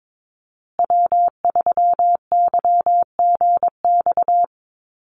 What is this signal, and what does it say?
Morse code, heard as a keyed tone, about 22 wpm, 700 Hz. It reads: W3YGX
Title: USA Call Signs Repeated 3x 22wpm